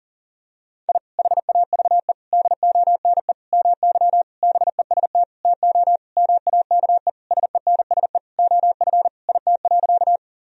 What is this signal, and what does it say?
Morse code, heard as a keyed tone, about 40 wpm, 700 Hz. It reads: I HAVE DONE MY BEST TO MAKE SENSE OF IT.
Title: Sentences from Top 1000 Words - Review Part 4 - 40wpm